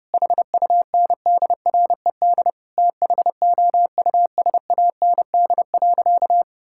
Morse code, heard as a keyed tone, about 30 wpm, 700 Hz. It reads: HUNDRED THOUSAND.
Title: Sentences from Top 1000 Words - Review Part 5 - 30wpm